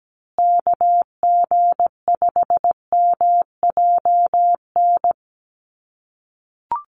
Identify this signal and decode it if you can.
Morse code, heard as a keyed tone, about 17 wpm, 700 Hz. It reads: KG5MJN
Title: USA Call Signs Repeated 2x 17wpm